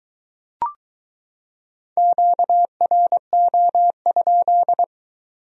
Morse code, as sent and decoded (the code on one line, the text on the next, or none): --.- .-. --- ..--..
QRO?